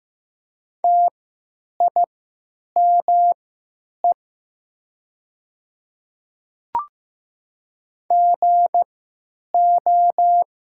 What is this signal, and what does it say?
Morse code, heard as a keyed tone, about 15 wpm, spaced as for 9 wpm, 700 Hz. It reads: TIME GO
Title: 3x Character Spacing - Top 300 Words - 15wpm